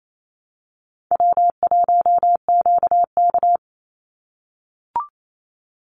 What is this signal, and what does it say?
Morse code, heard as a keyed tone, about 28 wpm, 700 Hz. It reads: W1QK
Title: Easy Call Signs - 28wpm